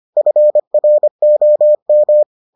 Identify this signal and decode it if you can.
Morse code, heard as a keyed tone, about 25 wpm, 600 Hz. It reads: FROM